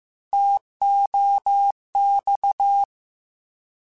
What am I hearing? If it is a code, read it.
Morse code, heard as a keyed tone, about 15 wpm, 800 Hz. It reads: TOX
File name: Contrasts / Contrasts-mp3s / 3LtrContrasts @13